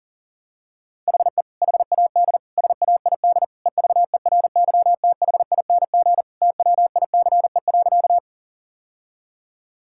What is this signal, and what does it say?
Morse code, heard as a keyed tone, about 40 wpm, 700 Hz. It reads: HE HAD SAID EVERYTHING TWICE.
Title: Sentences from Top 1600 Words 40wpm